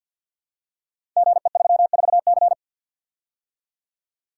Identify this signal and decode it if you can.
Morse code, heard as a keyed tone, about 46 wpm, 700 Hz. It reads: GE34C
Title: Code Groups - 5 Letter:Numbers - 50wpm